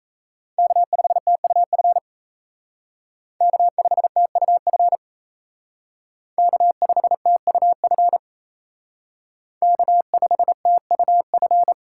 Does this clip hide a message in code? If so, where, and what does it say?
Yes: Morse code from 0.6 to 11.7 s, heard as a keyed tone, about 35 wpm, 700 Hz. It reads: K5TUF K5TUF K5TUF K5TUF